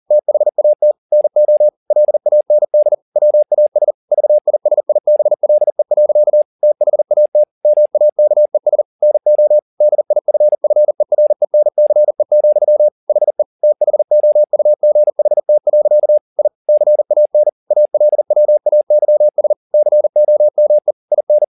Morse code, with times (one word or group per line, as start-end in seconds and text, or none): 0.1-0.9 THAT
1.1-1.7 NO
1.9-2.9 LAND
3.2-3.9 WAS
4.1-6.4 VISIBLE.
6.6-7.4 THAT
7.6-8.8 MAKES
9.0-9.6 NO
9.8-12.9 DIFFERENCE,
13.1-13.4 HE
13.6-16.2 THOUGHT.
16.4-16.5 I
16.7-17.5 CAN
17.7-19.5 ALWAYS
19.7-20.9 COME
21.1-21.4 IN